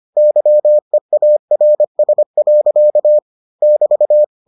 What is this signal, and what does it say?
Morse code, heard as a keyed tone, about 25 wpm, 600 Hz. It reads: YEARS. <BT>